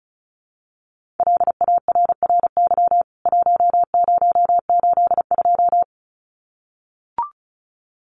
Morse code, heard as a keyed tone, about 34 wpm, 700 Hz. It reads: LARRY 1082
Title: CWT Exchange 35wpm